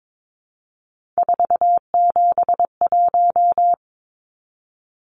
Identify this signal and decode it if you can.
Morse code, heard as a keyed tone, about 22 wpm, 700 Hz. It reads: 471
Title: Numbers - 3-Digit 22wpm